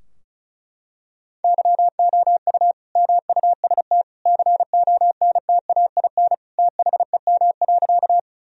Morse code, heard as a keyed tone, about 36 wpm, 700 Hz. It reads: YOU MUST CONTAIN THEM.
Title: Sentences from Top 1000 Words - Review Part 2 - 35wpm